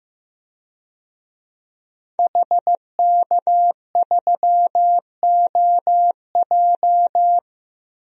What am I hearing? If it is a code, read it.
Morse code, heard as a keyed tone, about 15 wpm, 700 Hz. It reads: HK3OJ
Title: DXCC-100 Call Signs - 5 Characters - 15wpm